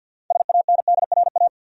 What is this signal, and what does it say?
Morse code, heard as a keyed tone, about 52 wpm, 700 Hz. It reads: SANDRA
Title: USA Common Names - 50wpm